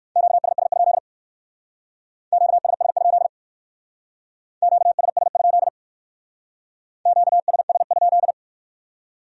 Morse code, heard as a keyed tone, about 54 wpm, 700 Hz. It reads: QSS? QSS? QSS? QSS?
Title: Speed-Racing Q-Codes - 45wpm